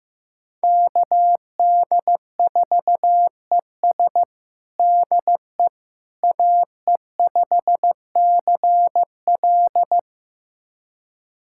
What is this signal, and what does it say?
Morse code, heard as a keyed tone, about 15 wpm, 700 Hz. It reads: KD4ES DE AE5CL
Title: USA Callsign DE Callsign - 5 Characters - 15wpm